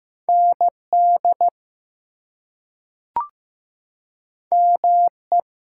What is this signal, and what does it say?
Morse code, heard as a keyed tone, about 15 wpm, 700 Hz. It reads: ND ME
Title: US States and Canadian Provinces 15wpm